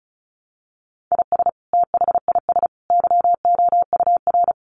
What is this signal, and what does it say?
Morse code, heard as a keyed tone, about 34 wpm, 700 Hz. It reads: IS THIS YOUR
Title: Sentences from Top 1500 Words 35wpm